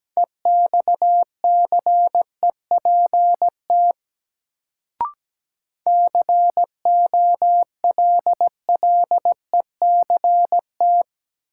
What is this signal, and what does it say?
Morse code, heard as a keyed tone, about 17 wpm, 700 Hz. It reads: EXCEPT COLLECT
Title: Sentences from Top 1600 Words - New Word Drill - 17wpm